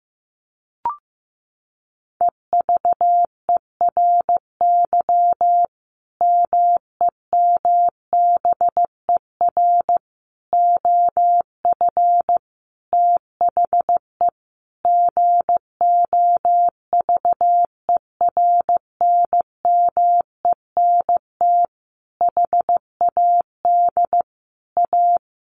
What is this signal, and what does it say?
Morse code, heard as a keyed tone, about 15 wpm, 700 Hz. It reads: EVERY MEMBER OF THE GOVERNMENT HAD A
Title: Sentences from Top 1400 Words 15wpm